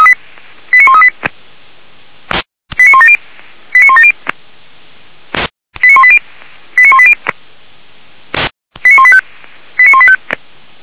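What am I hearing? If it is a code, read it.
Five-tone selective call (ZVEI), heard as quick runs of five tones, tones of about 70 ms. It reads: 89138 89170 89170 89180 89180 89185 89185
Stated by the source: ZVEI five-tone selective call received by SDR.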